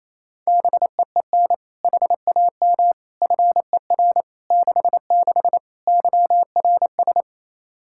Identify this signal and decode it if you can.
Morse code, heard as a keyed tone, about 28 wpm, 700 Hz. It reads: BEEN HAM FER 66 YRS
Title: Practice QSO - Exchange 3 - More Info - 28wpm